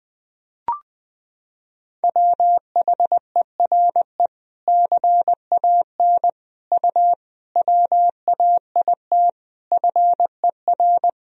Morse code, heard as a keyed tone, about 21 wpm, 700 Hz. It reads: WHERE CAN U WAIT FER
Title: Sentences from Top 500 Words and Abbreviations 20wpm